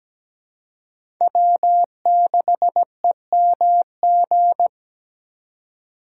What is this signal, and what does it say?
Morse code, heard as a keyed tone, about 20 wpm, 700 Hz. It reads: W6EMG
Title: Speed Racing - USA 1x3 Call Signs 17wpm